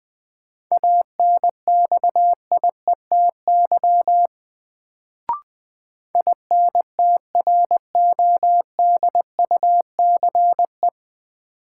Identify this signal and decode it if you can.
Morse code, heard as a keyed tone, about 20 wpm, 700 Hz. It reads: ANXIETY INTRODUCE
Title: Sentences from Top 1600 Words - New Word Drill - 20wpm